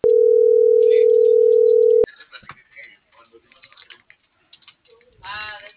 Ringback tone (North America), 440 + 480 Hz.